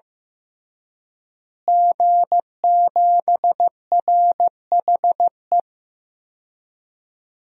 Morse code, heard as a keyed tone, about 15 wpm, 700 Hz. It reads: G7RHE
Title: DXCC-100 Call Signs - 5 Characters - Repeated 3x - 15wpm